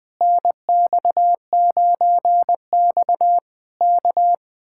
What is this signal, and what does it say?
Morse code, heard as a keyed tone, about 20 wpm, 700 Hz. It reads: NX9X K
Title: Speed-Racing Common QSO Phrases - 17wpm